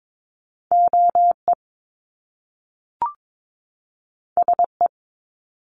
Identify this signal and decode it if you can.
Morse code, heard as a keyed tone, about 22 wpm, 700 Hz. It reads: OE SE
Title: Top Two Letter Combinations in Words 22wpm